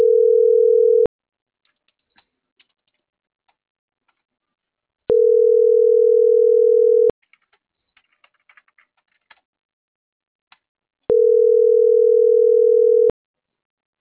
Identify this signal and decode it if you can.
Ringback tone (North America), 440 + 480 Hz.